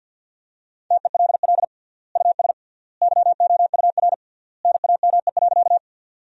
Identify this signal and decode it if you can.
Morse code, heard as a keyed tone, about 50 wpm, 700 Hz. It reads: TELL US YOUR NAME.
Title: 3x Word-Spacing - Sentences from Top 200 Words - 50wpm